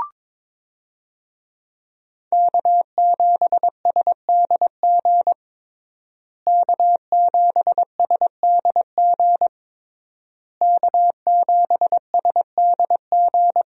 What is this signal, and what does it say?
Morse code, heard as a keyed tone, about 22 wpm, 700 Hz. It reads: K7SDG K7SDG K7SDG
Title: USA Call Signs - 5 Characters - Repeated 3x - 22wpm